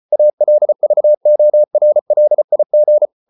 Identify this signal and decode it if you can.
Morse code, heard as a keyed tone, about 34 wpm, 600 Hz. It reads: ALVORLIG
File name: Ferie QRQ 34wpm